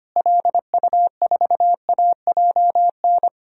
Morse code, heard as a keyed tone, about 25 wpm, 700 Hz. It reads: LU4AJN
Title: DXCC-100 Call Signs - 6 Characters - Repeated 2x - 25wpm